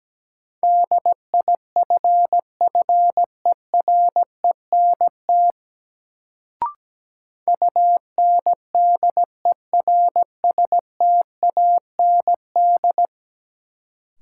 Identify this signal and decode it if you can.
Morse code, heard as a keyed tone, about 17 wpm, 700 Hz. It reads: DIFFERENT UNDERSTAND
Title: Sentences from Top 600 Words - New Word Drill - 17wpm